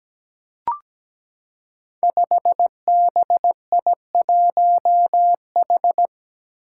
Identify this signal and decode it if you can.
Morse code, heard as a keyed tone, about 17 wpm, 700 Hz. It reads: 5BI1H
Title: Five Challenging Letter-Number Combinations 17wpm